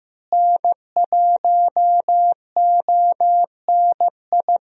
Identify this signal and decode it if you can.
Morse code, heard as a keyed tone, about 14 wpm, 700 Hz. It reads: N1ONI
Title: USA Call Signs - 5 Characters - Repeated 3x - 15wpm